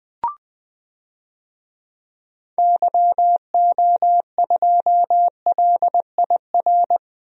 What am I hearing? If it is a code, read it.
Morse code, heard as a keyed tone, about 20 wpm, 700 Hz. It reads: YO2LIR
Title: DXCC-100 Call Signs - 6 Characters - 20wpm